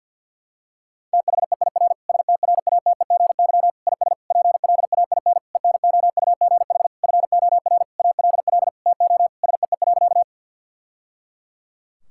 Morse code, heard as a keyed tone, about 52 wpm, 700 Hz. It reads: THEIR STRATEGY IS PLAIN ENOUGH FOR ALL TO SEE.
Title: Sentences from Top 1000 Words - Review Part 5 - 50wpm